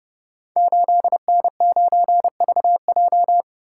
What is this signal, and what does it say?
Morse code, heard as a keyed tone, about 30 wpm, 700 Hz. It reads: 8N9VJ
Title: Code Groups - 5 Letter:Numbers - 30wpm